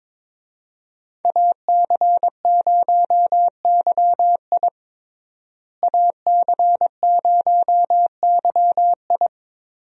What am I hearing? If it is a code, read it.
Morse code, heard as a keyed tone, about 22 wpm, 700 Hz. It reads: AC0YI AC0YI
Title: USA Call Signs - 5 Characters - Repeated 2x - 22wpm